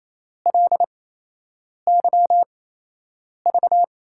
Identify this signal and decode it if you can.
Morse code, heard as a keyed tone, about 28 wpm, spaced as for 8 wpm, 700 Hz. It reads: LYV
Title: USA Call Signs - 8x Character Spacing - 28wpm